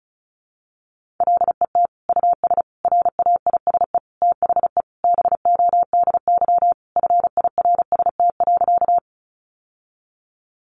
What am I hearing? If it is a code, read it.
Morse code, heard as a keyed tone, about 34 wpm, 700 Hz. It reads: LET US RAISE THE BODY FIRST.
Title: Sentences from Top 1400 Words 35wpm